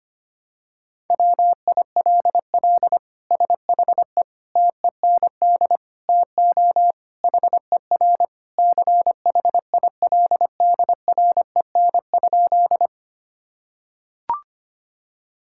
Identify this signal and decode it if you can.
Morse code, heard as a keyed tone, about 25 wpm, 700 Hz. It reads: WILL SHE TEND TO HER CHILDREN?
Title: Sentences from Top 1000 Words - Review Part 5 - 25wpm